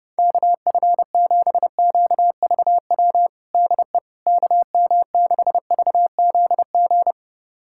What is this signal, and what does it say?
Morse code, heard as a keyed tone, about 30 wpm, 700 Hz. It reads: KF7QVW DE KM6VZG
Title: USA Callsign DE Callsign 30wpm